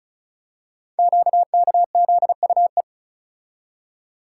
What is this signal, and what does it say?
Morse code, heard as a keyed tone, about 36 wpm, 690 Hz. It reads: QKZUE
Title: Code Groups - 5 Letters - 35wpm